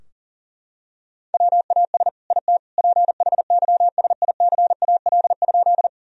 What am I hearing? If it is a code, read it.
Morse code, heard as a keyed tone, about 40 wpm, 700 Hz. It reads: WAS IT PHYSICAL?
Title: Sentences from Top 1000 Words - Review Part 4 - 40wpm